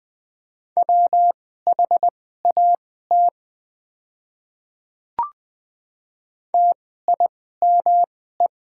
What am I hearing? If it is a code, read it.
Morse code, heard as a keyed tone, about 20 wpm, spaced as for 15 wpm, 700 Hz. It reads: WHAT TIME
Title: 2x Character Spacing - Top 100 Words - 20wpm